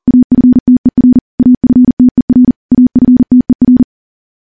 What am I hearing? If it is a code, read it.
Morse code, heard as a keyed tone, about 40 wpm, 250 Hz. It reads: AFTER AFTER AFTER
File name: morse-training-40WPM